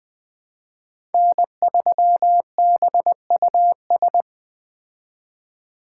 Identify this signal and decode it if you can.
Morse code, heard as a keyed tone, about 20 wpm, 700 Hz. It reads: N3BUS